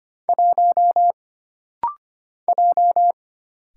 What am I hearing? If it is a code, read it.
Morse code, heard as a keyed tone, about 25 wpm, 700 Hz. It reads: 1 J
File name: Single Character (Letter:Number:Punctuation:Prosign) - Rapid-Fire - 25wpm